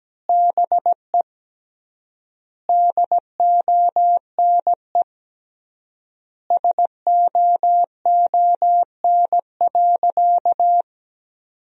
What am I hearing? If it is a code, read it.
Morse code, heard as a keyed tone, about 17 wpm, 700 Hz. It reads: BE DONE SOON.
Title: 3x Word-Spacing - Sentences from Top 500 Words - 17wpm